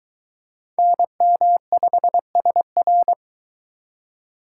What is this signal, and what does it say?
Morse code, heard as a keyed tone, about 23 wpm, 700 Hz. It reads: NM5SR